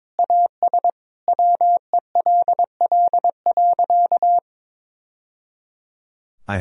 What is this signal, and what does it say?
Morse code, heard as a keyed tone, about 22 wpm, 700 Hz. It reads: AS WELL.
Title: Sentences from Top 1000 Words - Review Part 2 - 22wpm